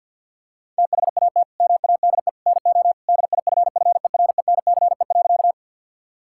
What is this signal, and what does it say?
Morse code, heard as a keyed tone, about 50 wpm, 700 Hz. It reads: THAT MADE NO DIFFERENCE.
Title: Sentences from Top 1000 Words - Review Part 5 - 50wpm